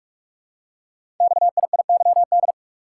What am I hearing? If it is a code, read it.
Morse code, heard as a keyed tone, about 44 wpm, 700 Hz. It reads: XIIYD
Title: Code Groups - 5 Letter:Numbers - 45wpm